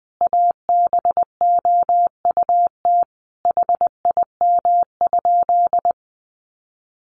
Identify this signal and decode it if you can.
Morse code, heard as a keyed tone, about 20 wpm, 700 Hz. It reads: ABOUT HIM?
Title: Sentences from Top 1000 Words 20wpm